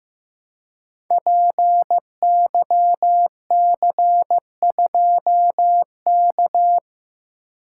Morse code, heard as a keyed tone, about 15 wpm, 700 Hz. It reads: PYC2K
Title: Code Groups - 5 Letter:Numbers - 15wpm